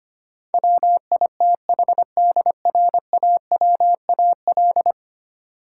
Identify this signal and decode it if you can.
Morse code, heard as a keyed tone, about 25 wpm, 700 Hz. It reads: WITHDRAWAL
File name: Ten Letter Words - 25wpm